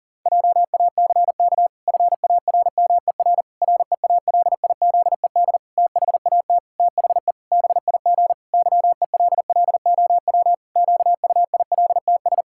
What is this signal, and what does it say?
Morse code, heard as a keyed tone, about 40 wpm, 700 Hz. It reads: JACK FARMER REALIZED THAT THE BIG YELLOW QUILTS
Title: Pangram Challenge v2 - 40wpm